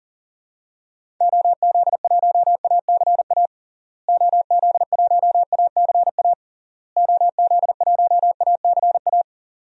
Morse code, heard as a keyed tone, about 40 wpm, 700 Hz. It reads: OZ1ACA OZ1ACA OZ1ACA
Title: DXCC-100 Call Signs - 6 Characters - Repeated 3x - 40wpm